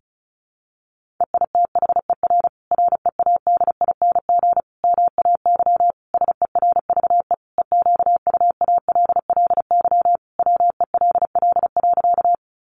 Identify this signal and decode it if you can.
Morse code, heard as a keyed tone, about 34 wpm, 700 Hz. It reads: EITHER READING MAY SERVE EQUALLY WELL.
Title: Sentences from Top 1400 Words 35wpm